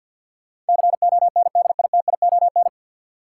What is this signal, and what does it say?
Morse code, heard as a keyed tone, about 46 wpm, 700 Hz. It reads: CONDITION
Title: Sentences from Top 1000 Words - New Word Drill - 50wpm